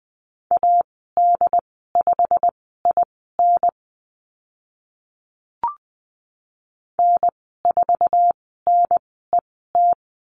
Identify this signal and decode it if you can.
Morse code, heard as a keyed tone, about 20 wpm, spaced as for 15 wpm, 700 Hz. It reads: AD5IN N4NET
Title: 088 Lesson 15g - Number 5 - Call Signs 20wpm